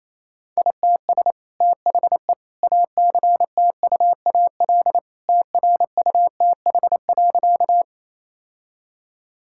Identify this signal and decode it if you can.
Morse code, heard as a keyed tone, about 27 wpm, 700 Hz. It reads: ITS THE ACTUAL TRUTH.
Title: Sentences from Top 1000 Words - Review Part 5 - 28wpm